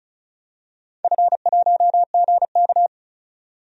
Morse code, heard as a keyed tone, about 36 wpm, 700 Hz. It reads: F1GK
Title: DXCC-100 Call Signs - 4 Characters - 35wpm